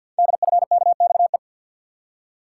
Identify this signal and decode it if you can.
Morse code, heard as a keyed tone, about 48 wpm, 700 Hz. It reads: DFKXE
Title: Code Groups - 5 Letters - 50wpm